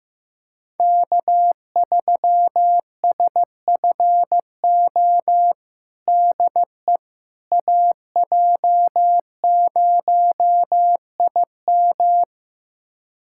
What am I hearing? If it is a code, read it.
Morse code, heard as a keyed tone, about 15 wpm, 700 Hz. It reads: K3SFO DE AJ0IM
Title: USA Callsign DE Callsign - 5 Characters - 15wpm